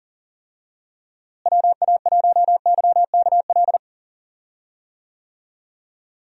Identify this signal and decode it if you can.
Morse code, heard as a keyed tone, about 40 wpm, 700 Hz. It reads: WA1YKL
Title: USA Call Signs - 6 Characters - Repeated 2x - 40wpm